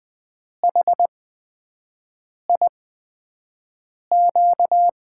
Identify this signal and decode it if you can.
Morse code, heard as a keyed tone, about 20 wpm, spaced as for 5 wpm, 700 Hz. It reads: HIQ
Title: USA Call Signs - 6 Characters - 8x Character Spacing - 20wpm